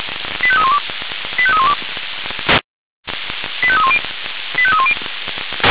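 Five-tone selective call (ZVEI), heard as quick runs of five tones, tones of about 70 ms. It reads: 85312 85312 85310 85310
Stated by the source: ZVEI five-tone selective call 85894 received with an SDR.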